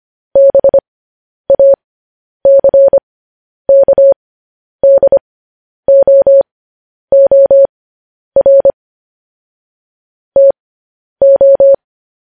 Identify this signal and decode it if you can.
Morse code, heard as a keyed tone, about 25 wpm, spaced as for 10 wpm, 550 Hz. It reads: BACKDOOR TO
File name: podcast / News headlines-1x-25x10